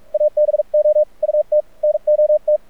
Morse code, heard as a keyed tone, about 51 wpm, spaced as for 39 wpm, 600 Hz, with radio noise. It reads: ABOUT NOT
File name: archived / CRSnet